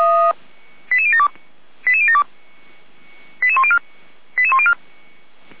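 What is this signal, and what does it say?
Five-tone selective call (ZVEI), heard as quick runs of five tones, tones of about 80 ms. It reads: 88072 88072 88104 88104
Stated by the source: ZVEI five-tone selective call, received on an SDR.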